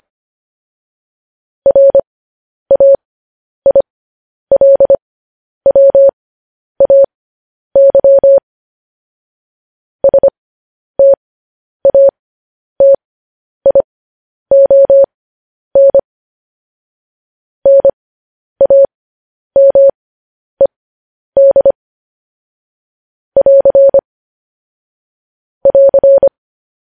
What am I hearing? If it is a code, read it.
Morse code, heard as a keyed tone, about 25 wpm, spaced as for 10 wpm, 550 Hz. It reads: RAILWAY STATION NAMED <AR> <AR>
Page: news headlines in morse code at 10 wpm